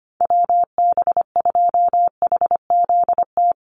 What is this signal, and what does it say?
Morse code, heard as a keyed tone, about 25 wpm, 700 Hz. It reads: WB2HZT
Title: USA Call Signs 25wpm